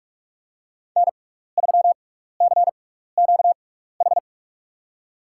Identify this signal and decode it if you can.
Morse code, heard as a keyed tone, about 47 wpm, spaced as for 16 wpm, 690 Hz. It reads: N3CQH
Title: USA Call Signs - 5 Characters - 6x Character Spacing - 45wpm